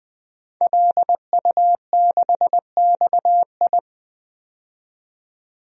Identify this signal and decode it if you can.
Morse code, heard as a keyed tone, about 20 wpm, 700 Hz. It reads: LU6XI
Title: DXCC-100 Call Signs - 5 Characters - 20wpm